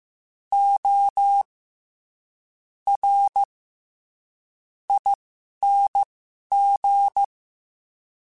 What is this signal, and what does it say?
Morse code, heard as a keyed tone, about 15 wpm, spaced as for 7 wpm, 800 Hz. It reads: ORING
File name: Special INGs-mp3s / Special INGs-mp3s / SlowToFaster ing @10